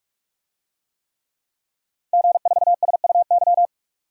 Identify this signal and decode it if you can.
Morse code, heard as a keyed tone, about 44 wpm, 700 Hz. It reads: G4SUY
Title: DXCC-100 Call Signs - 5 Characters - 45wpm